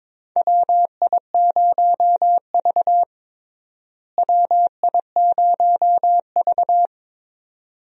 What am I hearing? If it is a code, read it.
Morse code, heard as a keyed tone, about 21 wpm, 700 Hz. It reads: WI0V WI0V
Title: USA Call Signs - 4 Characters - Repeated 2x - 22wpm